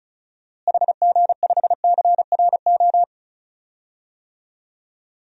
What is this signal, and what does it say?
Morse code, heard as a keyed tone, about 34 wpm, 700 Hz. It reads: HG5CRO
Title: DXCC-100 Call Signs - 6 Characters - Repeated 3x - 35wpm